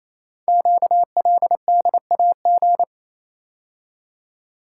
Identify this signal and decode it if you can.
Morse code, heard as a keyed tone, about 27 wpm, 700 Hz. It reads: QLDAG
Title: Code Groups - 5 Letters - 28wpm